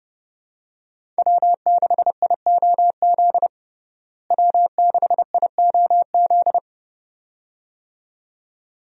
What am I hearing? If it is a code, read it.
Morse code, heard as a keyed tone, about 32 wpm, 700 Hz. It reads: W6IOZ W6IOZ
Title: USA Call Signs - 5 Characters - Repeated 2x - 30wpm